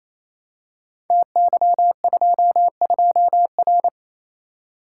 Morse code, heard as a keyed tone, about 29 wpm, 700 Hz. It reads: TY22R
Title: Code Groups - 5 Letter:Numbers - 28wpm